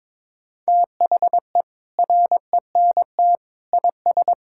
Morse code, heard as a keyed tone, about 21 wpm, 700 Hz. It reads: THE RENT IS
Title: Sentences from Top 1000 Words - Review Part 5 - 22wpm